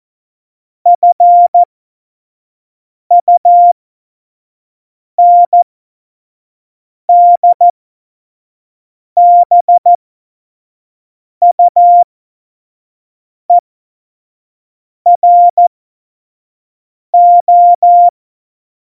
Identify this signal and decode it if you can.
Morse code, heard as a keyed tone, about 14 wpm, spaced as for 5 wpm, 700 Hz. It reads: FUNDBUERO
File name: Morsekurs / Witze / Witz 02. Oktober 2018, 25 F70